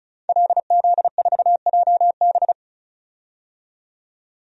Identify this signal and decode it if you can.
Morse code, heard as a keyed tone, about 36 wpm, 700 Hz. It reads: LZ4JB
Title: DXCC-100 Call Signs - 5 Characters - 35wpm